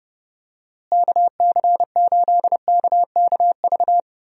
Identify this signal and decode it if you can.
Morse code, heard as a keyed tone, about 32 wpm, 700 Hz. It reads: KC8KKV
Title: USA Call Signs - 6 Characters - Repeated 2x - 30wpm